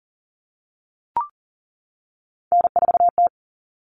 Morse code, heard as a keyed tone, about 40 wpm, 700 Hz. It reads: N4T
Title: All Cut Numbers - 3-Digit 40wpm